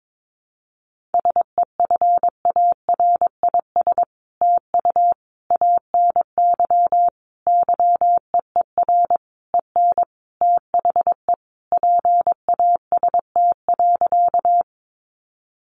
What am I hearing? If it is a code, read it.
Morse code, heard as a keyed tone, about 22 wpm, 700 Hz. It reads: SEFARIS TU ANY YEER EN THE PAST.
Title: A Sound Of Thunder 22wpm